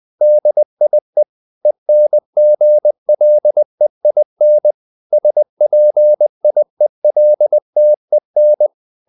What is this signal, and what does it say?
Morse code, heard as a keyed tone, about 20 wpm, 600 Hz. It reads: DIE ENGLEIN SPIELTEN